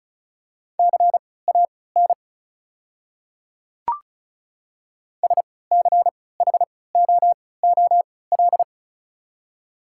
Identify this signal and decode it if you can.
Morse code, heard as a keyed tone, about 34 wpm, spaced as for 20 wpm, 700 Hz. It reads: CAN SCHOOL
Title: 3x Character Spacing - Top 300 Words - 35wpm